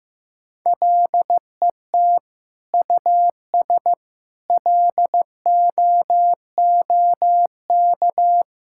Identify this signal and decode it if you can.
Morse code, heard as a keyed tone, about 15 wpm, 700 Hz. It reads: LET US LOOK
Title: Sentences from Top 1600 Words 15wpm